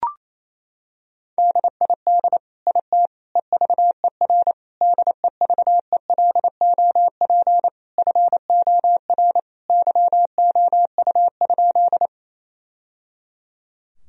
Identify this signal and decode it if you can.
Morse code, heard as a keyed tone, about 28 wpm, 700 Hz. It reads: DID IT EVER DEVELOP FOR YOU?
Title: Sentences from Top 1000 Words - Review Part 2 - 28wpm